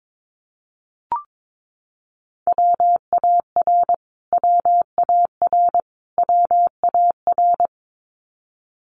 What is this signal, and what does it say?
Morse code, heard as a keyed tone, about 22 wpm, 700 Hz. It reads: WAR WAR WAR
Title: Top 500 Words Repeated 22wpm